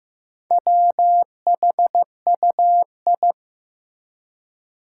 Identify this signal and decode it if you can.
Morse code, heard as a keyed tone, about 15 wpm, 700 Hz. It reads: WHUI
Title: Four Challenging Letter-Number Combinations 15wpm